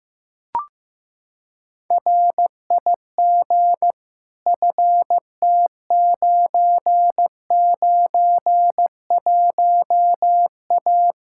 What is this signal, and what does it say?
Morse code, heard as a keyed tone, about 16 wpm, 700 Hz. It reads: RIG FT991A
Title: K1USN SST QSO Practice 15wpm